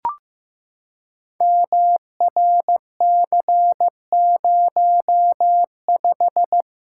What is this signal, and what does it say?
Morse code, heard as a keyed tone, about 15 wpm, 700 Hz. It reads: MRC05
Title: Code Groups - 5 Letter:Numbers - 15wpm